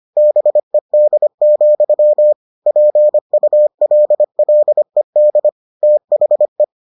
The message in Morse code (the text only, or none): BED, PULLED THE